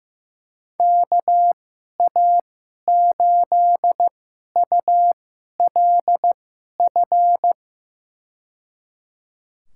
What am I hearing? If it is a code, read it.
Morse code, heard as a keyed tone, about 15 wpm, spaced as for 11 wpm, 700 Hz. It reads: KA8ULF
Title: USA Call Signs - 6 Characters - 2x Character Spacing - 15wpm